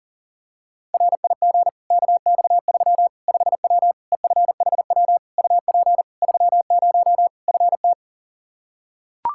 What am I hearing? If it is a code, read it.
Morse code, heard as a keyed tone, about 40 wpm, 700 Hz. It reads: RIG KX3 5W EFHW UP 30 FT
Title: Practice QSO - Exchange 2 - Info - 40wpm